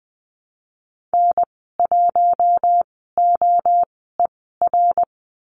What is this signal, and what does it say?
Morse code, heard as a keyed tone, about 20 wpm, spaced as for 14 wpm, 700 Hz. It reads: N1OER
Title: 055 Lesson 10g - Letter r - Call Signs 20wpm